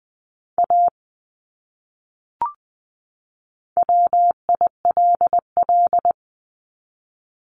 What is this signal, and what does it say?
Morse code, heard as a keyed tone, about 20 wpm, 700 Hz. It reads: A WILL
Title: Top 200 Words 20wpm